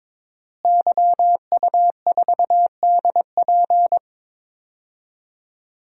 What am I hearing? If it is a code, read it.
Morse code, heard as a keyed tone, about 22 wpm, 700 Hz. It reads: YU4DP
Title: DXCC-100 Call Signs - 5 Characters - Repeated 2x - 22wpm